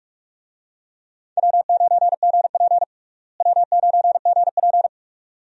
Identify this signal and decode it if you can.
Morse code, heard as a keyed tone, about 43 wpm, 700 Hz. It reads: W9GP W9GP
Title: USA Call Signs - 4 Characters - Repeated 2x - 45wpm